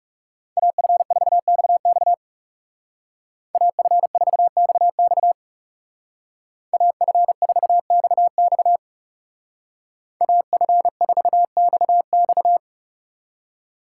Morse code, heard as a keyed tone, about 39 wpm, 700 Hz. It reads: AF4XX AF4XX AF4XX AF4XX